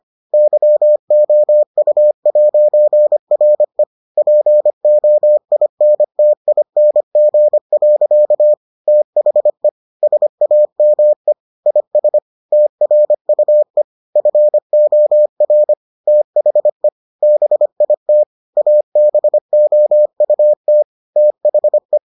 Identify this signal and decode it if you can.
Morse code, heard as a keyed tone, about 25 wpm, 600 Hz. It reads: YOU'RE POINTING. THE SAME IS TRUE FOR THE BIT ABOUT THE